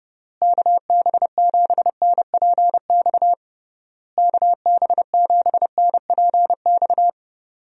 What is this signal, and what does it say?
Morse code, heard as a keyed tone, about 30 wpm, 700 Hz. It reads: KB7NPX KB7NPX
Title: USA Call Signs - 6 Characters - Repeated 3x - 30wpm